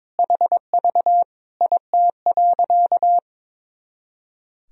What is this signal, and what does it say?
Morse code, heard as a keyed tone, about 22 wpm, 700 Hz. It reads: HV IT.
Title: Sentences from Top 100 Words and Abbreviations 22wpm